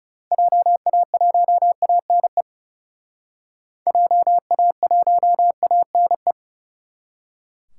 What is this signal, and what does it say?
Morse code, heard as a keyed tone, about 37 wpm, 700 Hz. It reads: JA1ANE JA1ANE
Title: Speed Racing - USA 2x3 Call Signs 30wpm